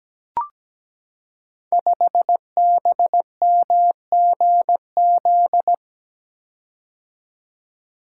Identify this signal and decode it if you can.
Morse code, heard as a keyed tone, about 17 wpm, 700 Hz. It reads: 5BMGZ
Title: Code Groups - 5 Letter:Numbers - 17wpm